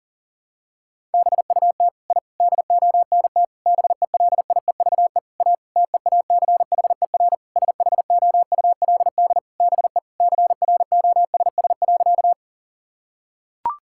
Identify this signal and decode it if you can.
Morse code, heard as a keyed tone, about 37 wpm, 700 Hz. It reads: BUT I DONT BELIEVE A TEACHER SHOULD BE CROSS.
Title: Sentences from Top 1000 Words - Review Part 6 - 40wpm